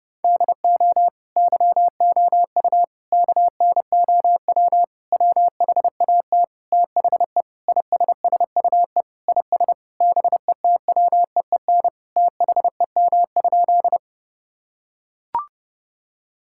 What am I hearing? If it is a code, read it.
Morse code, heard as a keyed tone, about 30 wpm, 700 Hz. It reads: DO YOU KNOW WHAT THE ISSUE IS BETWEEN THEM?
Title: Sentences from Top 1000 Words - Review Part 6 - 30wpm